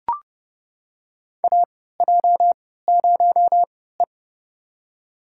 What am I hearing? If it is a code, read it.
Morse code, heard as a keyed tone, about 30 wpm, spaced as for 17 wpm, 700 Hz. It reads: AJ0E
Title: USA Call Signs - 3x Character Spacing - 30wpm